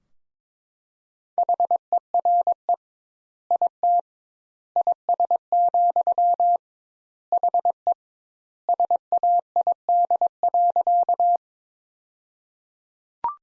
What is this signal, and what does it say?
Morse code, heard as a keyed tone, about 22 wpm, 700 Hz. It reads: HERE IT IS, HE SAID.
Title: Sentences Top 200 Words - 22wpm